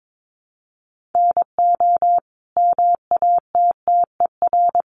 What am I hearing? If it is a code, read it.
Morse code, heard as a keyed tone, about 22 wpm, 700 Hz. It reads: NO MATTER
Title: Sentences from Top 1200 Words 22wpm